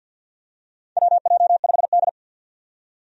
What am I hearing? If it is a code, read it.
Morse code, heard as a keyed tone, about 53 wpm, 700 Hz. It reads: WJ5D
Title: USA Call Signs - 4 Characters - 50wpm